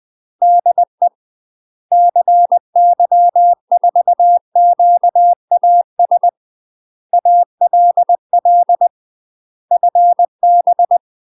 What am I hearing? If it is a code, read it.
Morse code, heard as a keyed tone, about 20 wpm, 700 Hz. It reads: DE CY4QAS ALL FB